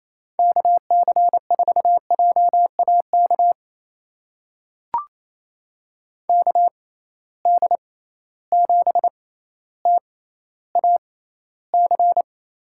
Morse code, heard as a keyed tone, about 28 wpm, spaced as for 14 wpm, 700 Hz. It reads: KC4JAK KD7TAC
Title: USA Call Signs - 6 Characters - 6x Character Spacing - 28wpm